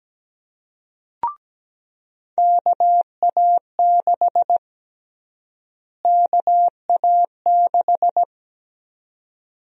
Morse code, heard as a keyed tone, about 17 wpm, 700 Hz. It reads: KA6 KA6
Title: USA Call Sign Prefixes - Repeated 2x - 17wpm